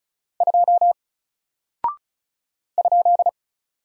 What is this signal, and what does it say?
Morse code, heard as a keyed tone, about 34 wpm, 700 Hz. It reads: 2 ?
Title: Single Character (Letter:Number:Punctuation:Prosign) Alt 35wpm